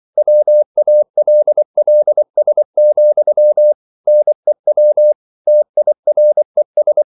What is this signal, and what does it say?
Morse code, heard as a keyed tone, about 24 wpm, 600 Hz. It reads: WALLS, NEW TIRES